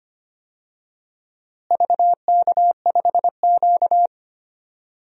Morse code, heard as a keyed tone, about 25 wpm, 700 Hz. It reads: VK5Q
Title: DXCC-100 Call Signs - 4 Characters - Repeated 2x - 25wpm